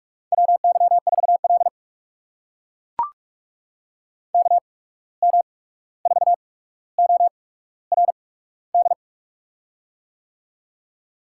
Morse code, heard as a keyed tone, about 44 wpm, 690 Hz. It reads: W Y 4 L K M 4 O R D
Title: USA Call Signs - 8x Character Spacing - 45wpm